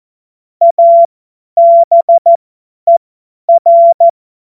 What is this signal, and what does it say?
Morse code, heard as a keyed tone, about 14 wpm, spaced as for 10 wpm, 680 Hz. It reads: ABER
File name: Morsekurs / Witze / Witz 30. Dezember 2025 50 F70